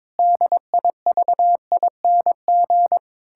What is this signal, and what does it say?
Morse code, heard as a keyed tone, about 22 wpm, 700 Hz. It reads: DIVING
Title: Pangram Challenge v2 - 22wpm